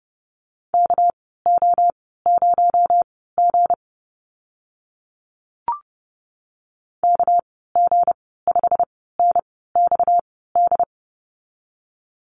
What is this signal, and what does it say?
Morse code, heard as a keyed tone, about 30 wpm, spaced as for 17 wpm, 700 Hz. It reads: KO0G KG5NXD
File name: Lesson 37g - Letter x - Call Signs 30wpm